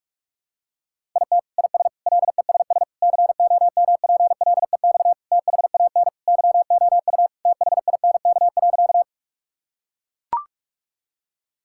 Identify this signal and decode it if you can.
Morse code, heard as a keyed tone, about 41 wpm, 700 Hz. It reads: IT IS LESS COMPLEX THAN YOU THINK.
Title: Sentences from Top 1000 Words - Review Part 6 - 45wpm